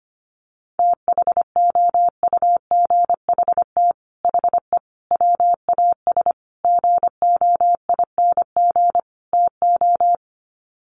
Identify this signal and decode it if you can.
Morse code, heard as a keyed tone, about 25 wpm, 700 Hz. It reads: THOUGHT HE WAS GOING TO
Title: Sets of 5 Words Encore2 25wpm